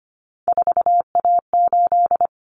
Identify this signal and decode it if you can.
Morse code, heard as a keyed tone, about 25 wpm, 700 Hz. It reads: 4A8
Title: Common Cut Numbers - 3-Digit 25wpm